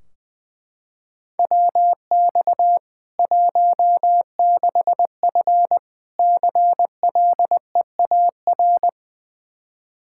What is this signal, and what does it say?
Morse code, heard as a keyed tone, about 20 wpm, 700 Hz. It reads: WX 16F CLEAR
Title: K1USN SST QSO Practice 20wpm